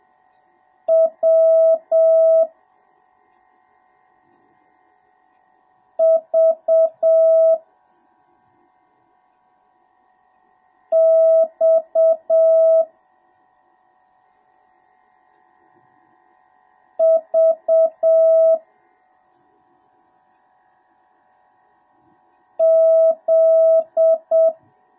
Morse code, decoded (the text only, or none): W V X V Z